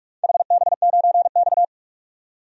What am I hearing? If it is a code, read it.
Morse code, heard as a keyed tone, about 46 wpm, 700 Hz. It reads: HB9X
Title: DXCC-100 Call Signs - 4 Characters - 45wpm